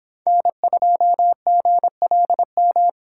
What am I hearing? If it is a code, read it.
Morse code, heard as a keyed tone, about 24 wpm, 700 Hz. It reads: N2GLM